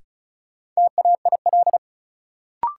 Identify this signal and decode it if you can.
Morse code, heard as a keyed tone, about 36 wpm, 700 Hz. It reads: TAIL
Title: Sentences from Top 1000 Words - Review Part 3 - New Word Drill - 35wpm